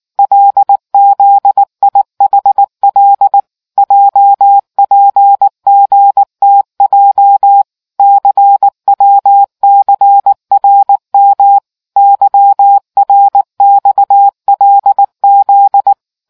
Morse code, heard as a keyed tone, about 19 wpm, 800 Hz. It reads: LZIHL JPGTJ CWCRM YRXLZ